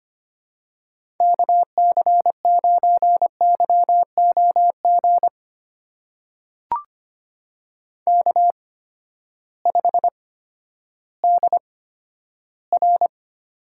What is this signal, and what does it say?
Morse code, heard as a keyed tone, about 25 wpm, spaced as for 12 wpm, 700 Hz. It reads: KC9YOG K5DR
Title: USA Call Signs - 8x Character Spacing - 25wpm